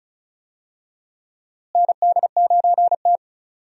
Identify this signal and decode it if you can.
Morse code, heard as a keyed tone, about 34 wpm, 700 Hz. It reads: ND9T
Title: USA Call Signs - 4 Characters - Repeated 2x - 35wpm